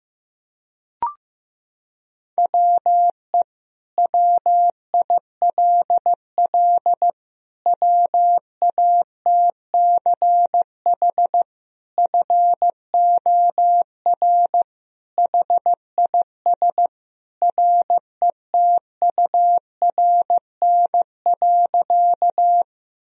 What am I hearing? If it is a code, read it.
Morse code, heard as a keyed tone, about 15 wpm, 700 Hz. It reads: WE WILL WATCH FOR HIS RETURN.
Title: Sentences from Top 1000 Words - Review Part 1 - 15wpm